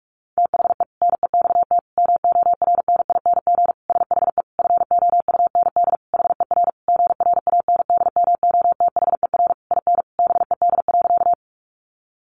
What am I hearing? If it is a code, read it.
Morse code, heard as a keyed tone, about 44 wpm, 700 Hz. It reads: THE NEXT MORNING SHE FOUND HER GRANDMOTHER IN BED.
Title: Sentences from Top 1200 Words 45wpm